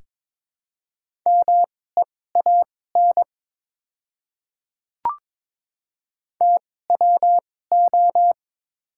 Morse code, heard as a keyed tone, about 23 wpm, spaced as for 16 wpm, 690 Hz. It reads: MEAN TWO
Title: 2x Character Spacing - Top 200 Words - 22wpm